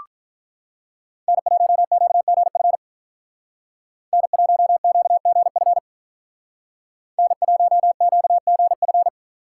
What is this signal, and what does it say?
Morse code, heard as a keyed tone, about 42 wpm, 700 Hz. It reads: N1QGF N1QGF N1QGF
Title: Speed Racing - USA 1x3 Call Signs 35wpm